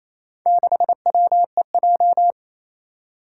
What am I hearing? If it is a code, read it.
Morse code, heard as a keyed tone, about 27 wpm, 700 Hz. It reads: 6WEJ
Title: Four Challenging Letter-Number Combinations 28wpm